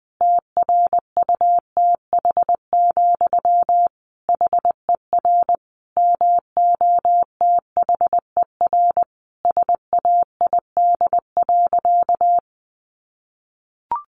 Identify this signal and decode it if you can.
Morse code, heard as a keyed tone, about 19 wpm, 700 Hz. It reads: TRUTH, HER MOTHER SAID.
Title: Sentences from Top 800 Words 20wpm